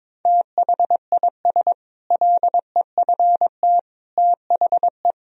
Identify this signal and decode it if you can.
Morse code, heard as a keyed tone, about 22 wpm, 700 Hz. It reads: THIS LEFT THE
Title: Sentences from Top 1000 Words - Review Part 4 - 22wpm